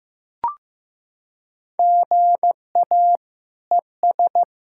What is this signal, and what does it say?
Morse code, heard as a keyed tone, about 16 wpm, 700 Hz. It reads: GA ES
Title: Practice QSO - Exchange 1 - Beginning - 15wpm